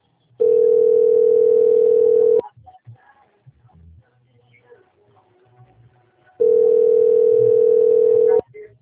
Ringback tone (North America), 440 + 480 Hz.